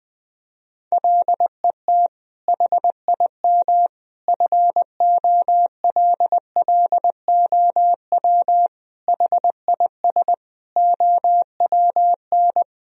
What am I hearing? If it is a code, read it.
Morse code, heard as a keyed tone, about 20 wpm, 700 Hz. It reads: LET HIM FOLLOW HIS OWN
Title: Sentences from Top 1000 Words - Review Part 1 - 20wpm